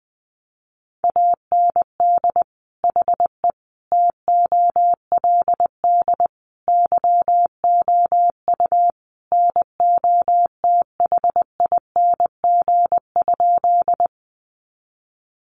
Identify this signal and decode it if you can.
Morse code, heard as a keyed tone, about 20 wpm, 700 Hz. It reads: AND HE TOLD YOU NOTHING?
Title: Sentences from Top 500 Words 20wpm